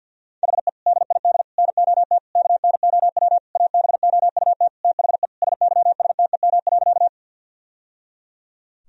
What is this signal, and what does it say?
Morse code, heard as a keyed tone, about 52 wpm, 700 Hz. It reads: HE DID NOT KNOW ABOUT THE SYSTEM.
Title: Sentences from Top 1000 Words - Review Part 3 - 50wpm